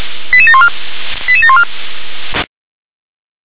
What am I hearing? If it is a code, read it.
Five-tone selective call (ZVEI), heard as quick runs of five tones, tones of about 70 ms. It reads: 88714 88714
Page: SDR/ZVEI